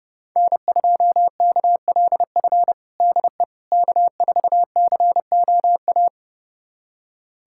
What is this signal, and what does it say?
Morse code, heard as a keyed tone, about 30 wpm, 700 Hz. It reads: N2KLF DE K4COA
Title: USA Callsign DE Callsign - 5 Characters - 30wpm